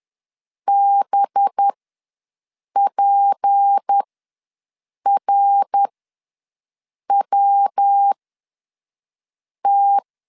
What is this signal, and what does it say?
Morse code, heard as a keyed tone, about 10 wpm, spaced as for 5 wpm, 800 Hz. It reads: BPRWT
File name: LEZIONI / 6 PKX0 / 6 P ws 10wpm 3751